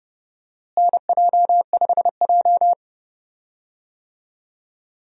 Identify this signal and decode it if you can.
Morse code, heard as a keyed tone, about 32 wpm, 700 Hz. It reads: NJ5J
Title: USA Call Signs - 4 Characters - Repeated 3x - 30wpm